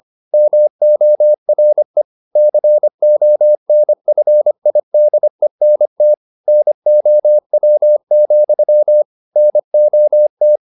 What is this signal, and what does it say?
Morse code, heard as a keyed tone, about 25 wpm, 600 Hz. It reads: MORE CONFIDENT NOW, NOT